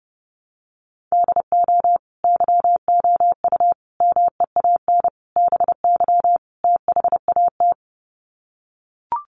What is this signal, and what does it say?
Morse code, heard as a keyed tone, about 30 wpm, 700 Hz. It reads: DO YOU MEAN BY THAT
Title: Sets of 5 Words Encore2 30wpm